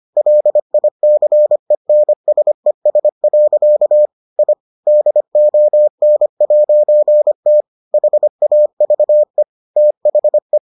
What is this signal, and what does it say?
Morse code, heard as a keyed tone, about 25 wpm, 600 Hz. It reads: LICENSES. I DON'T HAVE THE